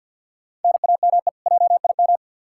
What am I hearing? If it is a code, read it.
Morse code, heard as a keyed tone, about 52 wpm, 700 Hz. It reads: NAME JIM
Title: K1USN SST QSO Practice 50wpm